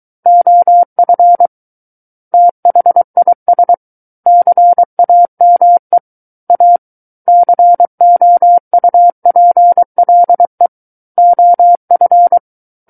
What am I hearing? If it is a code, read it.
Morse code, heard as a keyed tone, about 23 wpm, 700 Hz. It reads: OF THIS CAME A COUPLE OF
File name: Book One / 14-Chapter 14 - 23 WPM